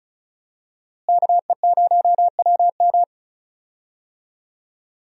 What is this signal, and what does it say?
Morse code, heard as a keyed tone, about 36 wpm, 700 Hz. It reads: KE0WM
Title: USA Call Signs - 5 Characters - Repeated 3x - 35wpm